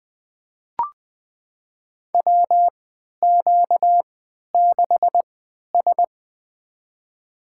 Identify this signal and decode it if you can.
Morse code, heard as a keyed tone, about 20 wpm, spaced as for 11 wpm, 700 Hz. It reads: WQ6S
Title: USA Call Signs - 3x Character Spacing - 20wpm